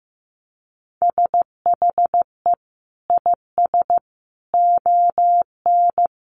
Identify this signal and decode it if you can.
Morse code, heard as a keyed tone, about 15 wpm, 700 Hz. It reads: SHE IS ON
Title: Sentences from Top 1200 Words 15wpm